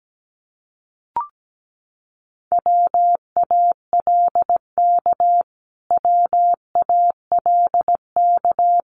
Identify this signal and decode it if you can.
Morse code, heard as a keyed tone, about 17 wpm, 700 Hz. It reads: WALK WALK
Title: Top 300 Words Repeated 17wpm